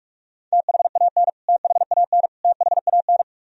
Morse code, heard as a keyed tone, about 46 wpm, 700 Hz. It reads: THAN THAN THAN
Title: Top 100 Words Repeated 45wpm